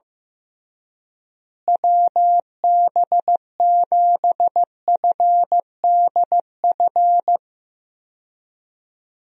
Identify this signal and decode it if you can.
Morse code, heard as a keyed tone, about 15 wpm, 700 Hz. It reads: WB7FDF
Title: USA Call Signs - 6 Characters - Repeated 2x - 15wpm